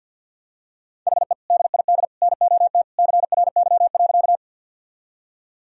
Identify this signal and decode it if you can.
Morse code, heard as a keyed tone, about 51 wpm, 700 Hz. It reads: HE DID NOT CRY.
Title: Sentences from Top 1000 Words - Review Part 2 - 50wpm